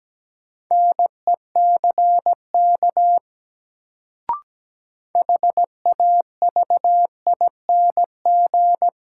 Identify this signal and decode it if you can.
Morse code, heard as a keyed tone, about 17 wpm, 700 Hz. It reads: NECK HAVING
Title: Sets of 5 Words Encore - New Word Drill - 17wpm